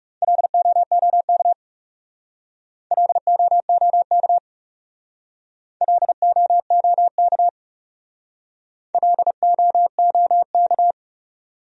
Morse code, heard as a keyed tone, about 38 wpm, 700 Hz. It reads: LOOK LOOK LOOK LOOK